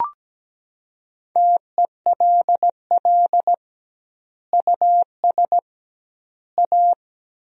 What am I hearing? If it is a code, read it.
Morse code, heard as a keyed tone, about 17 wpm, 700 Hz. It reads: TELL US A
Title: 2x Word-Spacing - Sentences from Top 300 Words - 17wpm